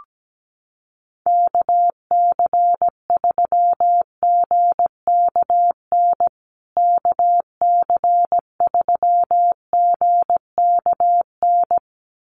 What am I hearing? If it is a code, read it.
Morse code, heard as a keyed tone, about 17 wpm, 700 Hz. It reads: KC3GKN KC3GKN
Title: USA Call Signs Repeated 2x 17wpm